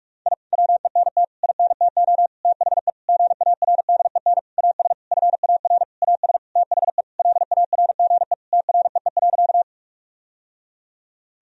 Morse code, heard as a keyed tone, about 41 wpm, 700 Hz. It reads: I WENT INTO THE GARDEN AS FAR AS THE LARGE TREE.
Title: Sentences from Top 1000 Words - Review Part 4 - 45wpm